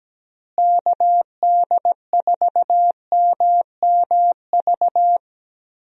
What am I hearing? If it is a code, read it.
Morse code, heard as a keyed tone, about 17 wpm, 700 Hz. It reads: KD4MMV
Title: USA Call Signs - 6 Characters - Repeated 3x - 17wpm